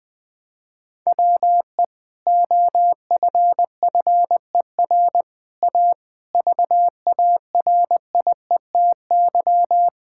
Morse code, heard as a keyed tone, about 20 wpm, 700 Hz. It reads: WE OFFER A VARIETY
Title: Sentences with 3-Word Phrases - 20wpm